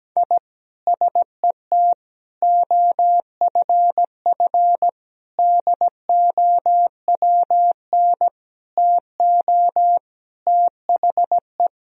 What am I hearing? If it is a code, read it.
Morse code, heard as a keyed tone, about 17 wpm, 700 Hz. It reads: I SET OFF DOWN TO THE
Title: Sentences from Top 1000 Words - Review Part 1 - 17wpm